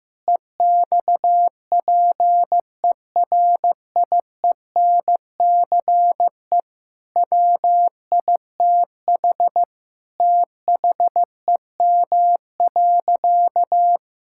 Morse code, heard as a keyed tone, about 15 wpm, 700 Hz. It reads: EXPERIENCE WITH THEM.
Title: Sentences from Top 1000 Words - Review Part 6 - 15wpm